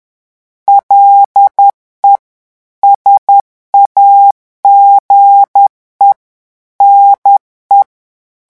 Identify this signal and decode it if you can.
Morse code, heard as a keyed tone, about 10 wpm, 800 Hz. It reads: LE SAGE NE
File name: morse mp3 / revision francais 10 mots minute